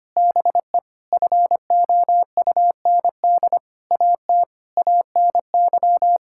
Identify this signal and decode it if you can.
Morse code, heard as a keyed tone, about 25 wpm, 700 Hz. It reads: BE FOUND AT ANY
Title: Sentences from Top 1000 Words - Review Part 5 - 25wpm